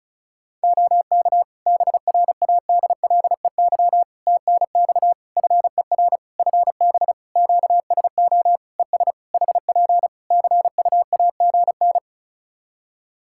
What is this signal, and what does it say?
Morse code, heard as a keyed tone, about 34 wpm, 700 Hz. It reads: OK BRADLEY TNX FER FB QSO ES HP CUAGN
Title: Practice QSO - Exchange 4 - Ending - 35wpm